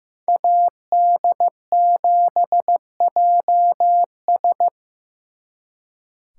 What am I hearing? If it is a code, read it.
Morse code, heard as a keyed tone, about 18 wpm, 700 Hz. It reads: AD7JS
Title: Speed Racing - USA 2x2 Call Signs 15wpm